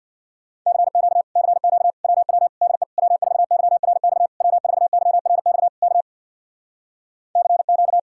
Morse code, heard as a keyed tone, about 65 wpm, 700 Hz. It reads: CQ CQ WW DE W4YRX W4YRX K CQ